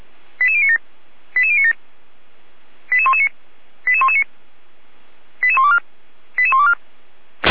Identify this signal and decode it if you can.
Five-tone selective call (ZVEI), heard as quick runs of five tones, tones of about 70 ms. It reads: 88097 88097 88119 88119 88124 88124
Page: SDR/ZVEI